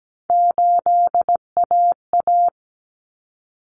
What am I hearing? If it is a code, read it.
Morse code, heard as a keyed tone, about 17 wpm, 690 Hz. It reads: 8AA
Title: Common Cut Numbers - 3-Digit 17wpm